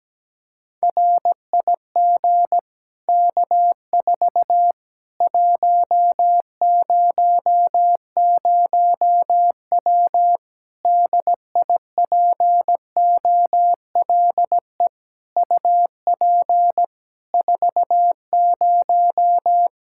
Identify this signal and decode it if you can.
Morse code, heard as a keyed tone, about 17 wpm, 700 Hz. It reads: RIG K4 100W DIPOLE UP 40
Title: Practice QSO - Exchange 2 - Info - 17wpm